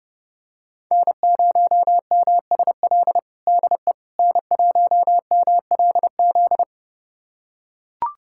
Morse code, heard as a keyed tone, about 32 wpm, 700 Hz. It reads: N0MSL DE N1MLZ
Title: USA Callsign DE Callsign - 5 Characters - 30wpm